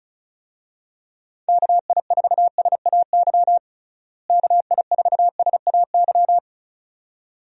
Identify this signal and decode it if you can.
Morse code, heard as a keyed tone, about 36 wpm, 700 Hz. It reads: KI4SAY KI4SAY
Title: USA Call Signs - 6 Characters - Repeated 2x - 35wpm